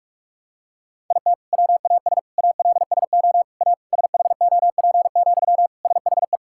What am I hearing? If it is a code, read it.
Morse code, heard as a keyed tone, about 46 wpm, 700 Hz. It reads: IT WAS ALSO A SHOP, SHE
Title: Sentences from Top 1600 Words 45wpm